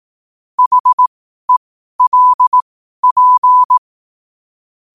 Morse code, heard as a keyed tone, about 18 wpm, spaced as for 12 wpm, 1000 Hz. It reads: HELP
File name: qotdfiles / 12wpm